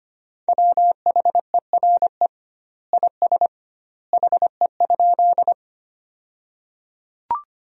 Morse code, heard as a keyed tone, about 25 wpm, 700 Hz. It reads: WHERE IS HE?
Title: 2x Word-Spacing - Sentences from Top 200 Words - 25wpm